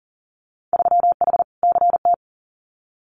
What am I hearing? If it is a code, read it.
Morse code, heard as a keyed tone, about 40 wpm, 700 Hz. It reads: 3H CT
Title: Winter Field Day Exchanges Encore 40wpm